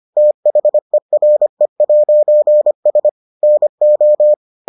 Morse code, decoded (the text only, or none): THERE'S NO